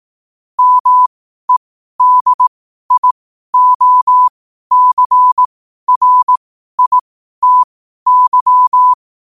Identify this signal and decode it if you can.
Morse code, heard as a keyed tone, about 18 wpm, spaced as for 12 wpm, 1000 Hz. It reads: MEDIOCRITY
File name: qotdfiles / 12wpm